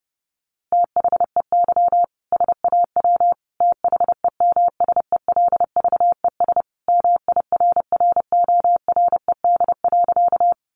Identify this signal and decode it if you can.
Morse code, heard as a keyed tone, about 30 wpm, 700 Hz. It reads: THEY SAW THEMSELVES MIRRORED.
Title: A Sound Of Thunder 30wpm